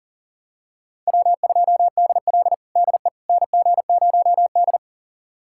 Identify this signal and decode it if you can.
Morse code, heard as a keyed tone, about 40 wpm, 700 Hz. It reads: W2DL DE NG0D
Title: USA Callsign DE Callsign - 4 Characters - 40wpm